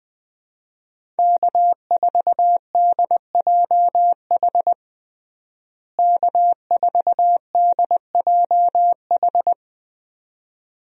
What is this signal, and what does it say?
Morse code, heard as a keyed tone, about 21 wpm, 700 Hz. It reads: K4DJH K4DJH
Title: USA Call Signs - 5 Characters - Repeated 2x - 20wpm